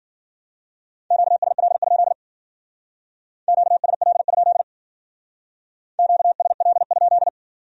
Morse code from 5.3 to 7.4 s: QSL?

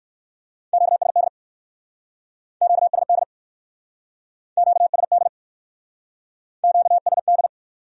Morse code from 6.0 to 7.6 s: QSD

